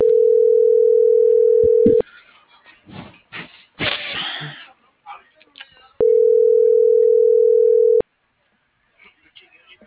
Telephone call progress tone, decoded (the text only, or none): ringback tone